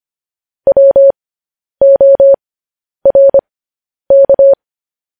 Morse code, heard as a keyed tone, about 25 wpm, spaced as for 10 wpm, 550 Hz. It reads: WORK
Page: News Headlines in Morse Code at 10 WPM